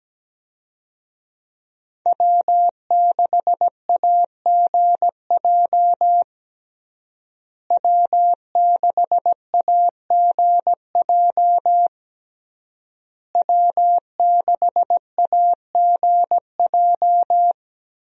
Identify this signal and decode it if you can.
Morse code, heard as a keyed tone, about 17 wpm, 700 Hz. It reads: W6AGJ W6AGJ W6AGJ
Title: USA Call Signs - 5 Characters - Repeated 3x - 17wpm